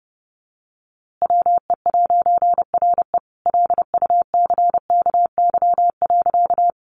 Morse code, heard as a keyed tone, about 30 wpm, 700 Hz. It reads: WE'RE LUCKY.
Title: A Sound Of Thunder 30wpm